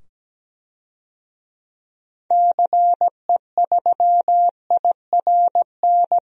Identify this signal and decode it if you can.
Morse code, heard as a keyed tone, about 17 wpm, 700 Hz. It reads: CE3IRN
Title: DXCC-100 Call Signs - 6 Characters - 17wpm